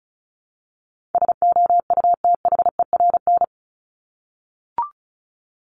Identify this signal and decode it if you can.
Morse code, heard as a keyed tone, about 34 wpm, 700 Hz. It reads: SOUTHERN
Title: Top 1751-2000 Words 35wpm